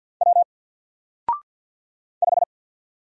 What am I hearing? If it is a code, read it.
Morse code, heard as a keyed tone, about 48 wpm, 700 Hz. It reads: W 5
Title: Single Character (Letter:Number:Punctuation:Prosign) 50wpm